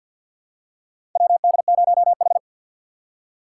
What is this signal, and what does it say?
Morse code, heard as a keyed tone, about 47 wpm, 700 Hz. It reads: WD0H
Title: USA Call Signs - 4 Characters - 2x Character Spacing - 50wpm